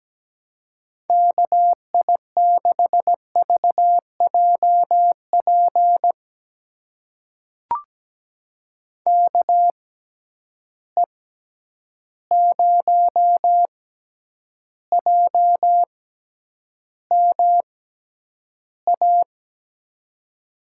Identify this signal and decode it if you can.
Morse code, heard as a keyed tone, about 17 wpm, spaced as for 9 wpm, 700 Hz. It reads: KI6VJP KE0JMA
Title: USA Call Signs - 6 Characters - 6x Character Spacing - 17wpm